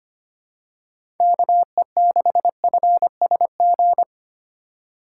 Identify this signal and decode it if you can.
Morse code, heard as a keyed tone, about 25 wpm, 700 Hz. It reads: KE6FSG
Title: USA Call Signs - 4x Character Spacing - 25wpm